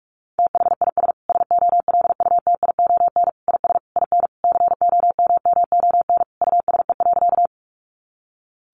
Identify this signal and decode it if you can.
Morse code, heard as a keyed tone, about 44 wpm, 700 Hz. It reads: THIS SOLUTION IS IN COMMON USE.
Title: Sentences from Top 1300 Words 45wpm